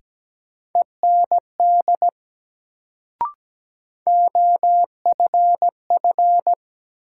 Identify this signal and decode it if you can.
Morse code, heard as a keyed tone, about 17 wpm, 700 Hz. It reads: END OFF
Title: Sentences from Top 200 Words - New Word Drill - 17wpm